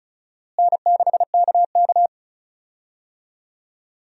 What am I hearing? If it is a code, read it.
Morse code, heard as a keyed tone, about 36 wpm, 700 Hz. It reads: N6KK
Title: USA Call Signs - 4 Characters - Repeated 2x - 35wpm